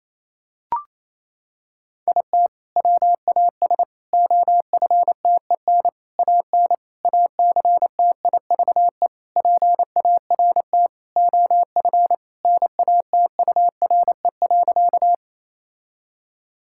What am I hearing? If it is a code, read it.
Morse code, heard as a keyed tone, about 28 wpm, 700 Hz. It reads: IT WAS OFTEN AN ACTIVE PART OF NATURE.
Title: Sentences from Top 1000 Words - Review Part 6 - 28wpm